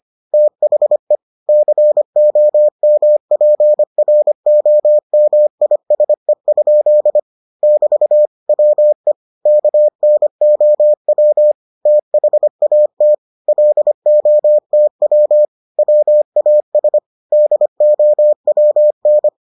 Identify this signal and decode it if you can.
Morse code, heard as a keyed tone, about 25 wpm, 600 Hz. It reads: THE COMPROMISE? <BT> WE KNOW THAT LOTW WAS DOWN